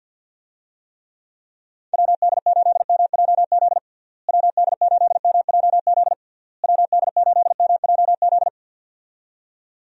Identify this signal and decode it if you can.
Morse code, heard as a keyed tone, about 50 wpm, 700 Hz. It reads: WD8MJZ WD8MJZ WD8MJZ
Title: USA Call Signs - 6 Characters - Repeated 3x - 50wpm